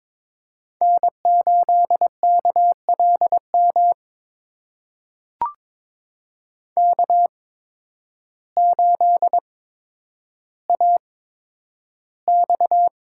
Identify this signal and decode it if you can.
Morse code, heard as a keyed tone, about 22 wpm, spaced as for 10 wpm, 700 Hz. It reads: N8KLM K8AX
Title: USA Call Signs - 5 Characters - 8x Character Spacing - 22wpm